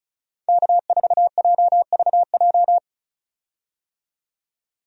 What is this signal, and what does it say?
Morse code, heard as a keyed tone, about 36 wpm, 700 Hz. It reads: K4JVJ
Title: USA Call Signs - 5 Characters - 35wpm